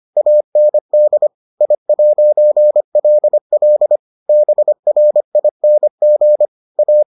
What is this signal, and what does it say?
Morse code, heard as a keyed tone, about 25 wpm, 600 Hz. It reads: AND I'LL BRING A